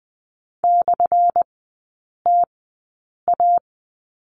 Morse code, heard as a keyed tone, about 20 wpm, spaced as for 8 wpm, 700 Hz. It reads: /TA
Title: 178 Lesson 29e - Slash - Three Characters 20wpm